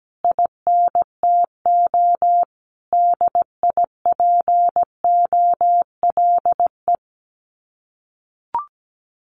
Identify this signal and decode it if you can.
Morse code, heard as a keyed tone, about 17 wpm, 700 Hz. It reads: INTO DIPOLE
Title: Common QSO Phrases Encore 17wpm